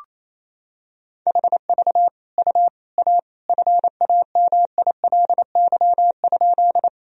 Morse code, heard as a keyed tone, about 28 wpm, 700 Hz. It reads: HV U A FAMILY?
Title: Sentences from Top 400 Words and Abbreviations 28wpm